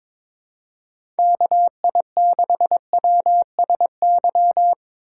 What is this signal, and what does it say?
Morse code, heard as a keyed tone, about 22 wpm, 700 Hz. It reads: KI6WSY
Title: USA Call Signs - 6 Characters - 3x Character Spacing - 22wpm